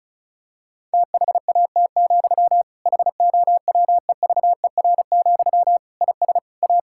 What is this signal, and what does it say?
Morse code, heard as a keyed tone, about 36 wpm, 700 Hz. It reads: THAT, HOWEVER, IS A
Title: Sentences from Top 1000 Words - Review Part 6 - 35wpm